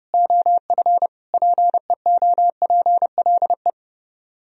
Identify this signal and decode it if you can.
Morse code, heard as a keyed tone, about 36 wpm, 700 Hz. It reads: OF PEOPLE
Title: Speed-Racing - Sets of 5 Words 30wpm